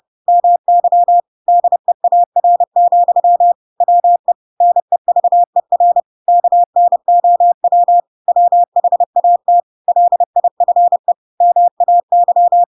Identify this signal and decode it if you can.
Morse code, heard as a keyed tone, about 30 wpm, 700 Hz. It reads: MY DEAR, WE NEVER KNOW WHAT LIFE MAY
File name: Pinocchio / Pinocchio ch7 30 30 ik1hge